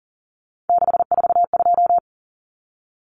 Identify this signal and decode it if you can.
Morse code, heard as a keyed tone, about 40 wpm, 700 Hz. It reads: 642
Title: Numbers - 3-Digit 40wpm